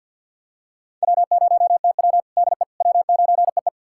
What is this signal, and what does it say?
Morse code, heard as a keyed tone, about 49 wpm, 700 Hz. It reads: W0TW DE W9EE
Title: USA Callsign DE Callsign - 4 Characters - 50wpm